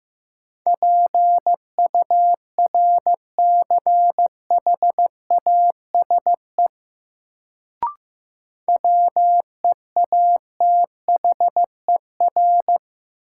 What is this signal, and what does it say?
Morse code, heard as a keyed tone, about 15 wpm, 700 Hz. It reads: PURCHASE WEATHER
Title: Sentences from Top 800 Words - New Word Drill - 15wpm